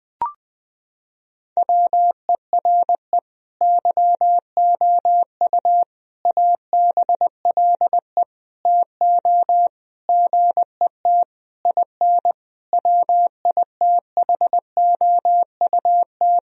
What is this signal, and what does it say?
Morse code, heard as a keyed tone, about 20 wpm, 700 Hz. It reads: WERE YOU ABLE TO GET IN WITHOUT
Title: Sentences from Top 1000 Words - Review Part 4 - 20wpm